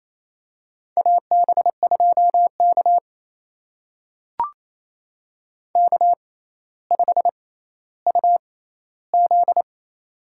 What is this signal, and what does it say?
Morse code, heard as a keyed tone, about 28 wpm, spaced as for 14 wpm, 700 Hz. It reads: AB2K K5UZ
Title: USA Call Signs - 4 Characters - 6x Character Spacing - 28wpm